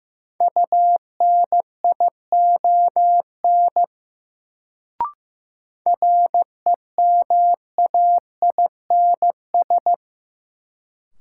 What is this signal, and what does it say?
Morse code, heard as a keyed tone, about 15 wpm, 700 Hz. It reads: UNION REMAINS
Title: Sets of 5 Words Encore - New Word Drill - 15wpm